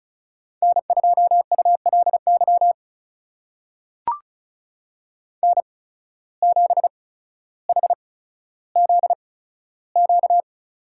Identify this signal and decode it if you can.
Morse code, heard as a keyed tone, about 34 wpm, spaced as for 15 wpm, 700 Hz. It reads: N2ULY N7HZQ
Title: USA Call Signs - 5 Characters - 8x Character Spacing - 35wpm